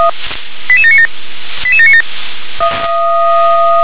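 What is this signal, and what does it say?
Five-tone selective call (ZVEI), heard as quick runs of five tones, tones of about 70 ms. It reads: 88797 88797
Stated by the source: ZVEI five-tone call 88727 received via SDR.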